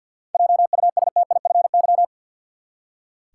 Morse code, heard as a keyed tone, about 46 wpm, 700 Hz. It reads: JUSTIFY
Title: Seven Letter Words - 50wpm